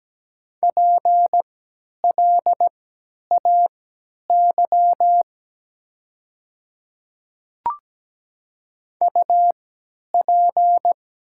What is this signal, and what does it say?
Morse code, heard as a keyed tone, about 17 wpm, spaced as for 10 wpm, 700 Hz. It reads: PLAY UP
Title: 3x Character Spacing - Top 200 Words - 17wpm